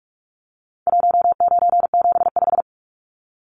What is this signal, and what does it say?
Morse code, heard as a keyed tone, about 44 wpm, 700 Hz. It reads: 1975
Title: Numbers - 4-Digit 45wpm